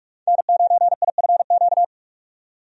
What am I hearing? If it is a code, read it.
Morse code, heard as a keyed tone, about 47 wpm, 700 Hz. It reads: N9IFQ
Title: USA Call Signs - 5 Characters - 45wpm